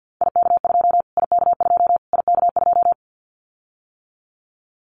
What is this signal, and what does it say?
Morse code, heard as a keyed tone, about 50 wpm, 700 Hz. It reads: IK2 IK2 IK2
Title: Callsign Prefixes for Top 25 DXCC - Repeated 3x - 50wpm